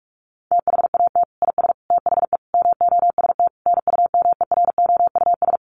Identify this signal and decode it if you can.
Morse code, heard as a keyed tone, about 44 wpm, 700 Hz. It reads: THAT IS THE MOST NUMEROUS
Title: Sentences from Top 1400 Words 45wpm